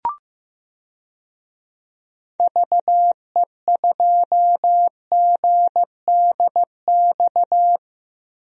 Morse code, heard as a keyed tone, about 15 wpm, 700 Hz. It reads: VE2GDX
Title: DXCC-100 Call Signs - 6 Characters - Repeated 3x - 15wpm